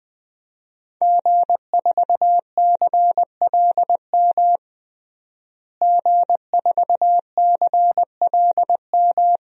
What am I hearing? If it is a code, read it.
Morse code, heard as a keyed tone, about 21 wpm, 700 Hz. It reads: G4CLM G4CLM
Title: DXCC-100 Call Signs - 5 Characters - Repeated 3x - 20wpm